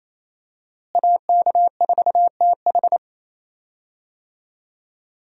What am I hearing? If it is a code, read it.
Morse code, heard as a keyed tone, about 28 wpm, 700 Hz. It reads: AK4TH
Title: USA Call Signs - 5 Characters - Repeated 3x - 28wpm